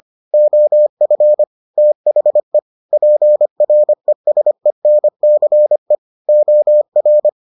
Morse code, heard as a keyed tone, about 25 wpm, 600 Hz. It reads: OF THE PRESENCE OR